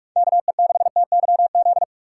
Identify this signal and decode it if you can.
Morse code, heard as a keyed tone, about 56 wpm, 700 Hz. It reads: KE6TYZ
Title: Speed Racing - USA Call Signs 45wpm